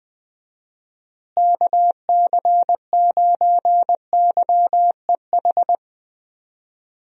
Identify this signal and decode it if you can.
Morse code, heard as a keyed tone, about 20 wpm, 700 Hz. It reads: KC9YEH
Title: USA Call Signs - 6 Characters - Repeated 2x - 20wpm